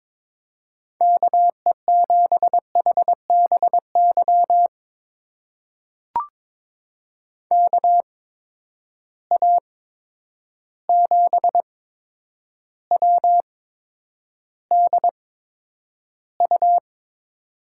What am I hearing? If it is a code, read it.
Morse code, heard as a keyed tone, about 21 wpm, spaced as for 9 wpm, 700 Hz. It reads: KE7HBY KA7WDU
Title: USA Call Signs - 6 Characters - 8x Character Spacing - 22wpm